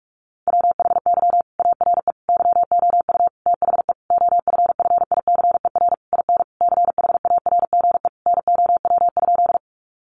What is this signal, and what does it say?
Morse code, heard as a keyed tone, about 44 wpm, 700 Hz. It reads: WHY ARE YOU THE OFFICER IN CHARGE NOW?
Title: Sentences from Top 1200 Words 45wpm